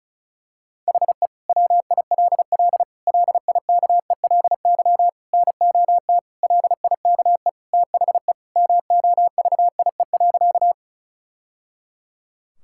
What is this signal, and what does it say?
Morse code, heard as a keyed tone, about 36 wpm, 700 Hz. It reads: HE WILL LIKELY NOT LIKE THE MOVIE.
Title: Sentences from Top 1000 Words - Review Part 5 - 35wpm